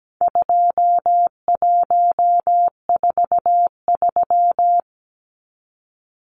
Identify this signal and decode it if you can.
Morse code, heard as a keyed tone, about 17 wpm, 700 Hz. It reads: 2143
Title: Numbers - 4-Digit 17wpm